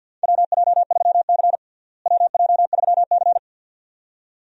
Morse code, heard as a keyed tone, about 49 wpm, 700 Hz. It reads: WJ3C WJ3C
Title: USA Call Signs - 4 Characters - Repeated 2x - 50wpm